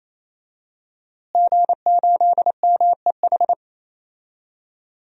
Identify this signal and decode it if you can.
Morse code, heard as a keyed tone, about 27 wpm, 700 Hz. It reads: G8MEH
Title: DXCC-100 Call Signs - 5 Characters - Repeated 3x - 28wpm